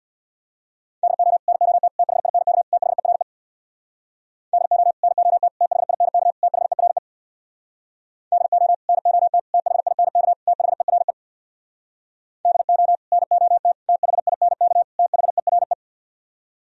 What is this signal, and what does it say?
Morse code, heard as a keyed tone, about 64 wpm, 700 Hz. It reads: DO NOT THINK THERE DO NOT THINK THERE DO NOT THINK THERE DO NOT THINK THERE